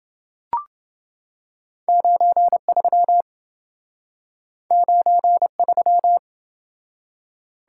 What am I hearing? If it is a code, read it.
Morse code, heard as a keyed tone, about 27 wpm, 700 Hz. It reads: 93 93